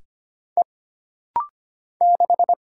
Morse code, heard as a keyed tone, about 25 wpm, 700 Hz. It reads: E 6
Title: Single Letter-Number - Rapid-Fire 25wpm